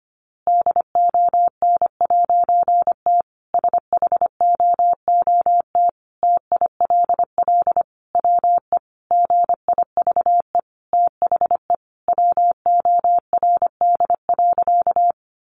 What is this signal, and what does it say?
Morse code, heard as a keyed tone, about 25 wpm, 700 Hz. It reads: DON'T SHOOT TILL WE GIVE THE WORD.
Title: A Sound Of Thunder 25wpm